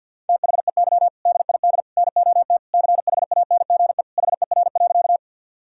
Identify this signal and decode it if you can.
Morse code, heard as a keyed tone, about 52 wpm, 700 Hz. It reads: THEY DID NOT CHANGE HER.
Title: Sentences from Top 1000 Words - Review Part 1 - 50wpm